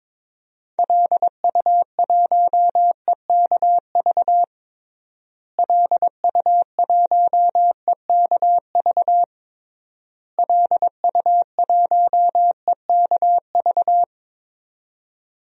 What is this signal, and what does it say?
Morse code, heard as a keyed tone, about 22 wpm, 700 Hz. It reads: LU1EKV LU1EKV LU1EKV
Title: DXCC-100 Call Signs - 6 Characters - Repeated 3x - 22wpm